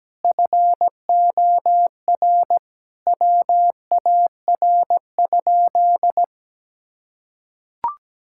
Morse code, heard as a keyed tone, about 17 wpm, 700 Hz. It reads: FOR WAR?
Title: Sentences from Top 1600 Words 17wpm